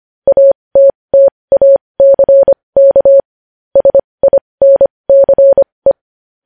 Morse code, heard as a keyed tone, about 25 wpm, spaced as for 20 wpm, 550 Hz. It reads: ATTACK SINCE
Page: news-headlines-in-morse-code-at-20-wpm